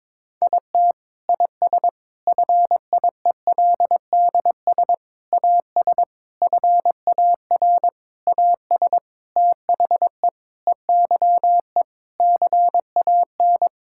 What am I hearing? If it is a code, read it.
Morse code, heard as a keyed tone, about 22 wpm, 700 Hz. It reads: IT IS FIELDS AS FAR AS THE EYE CAN
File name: Sentences with 7-Word Phrases - 22wpm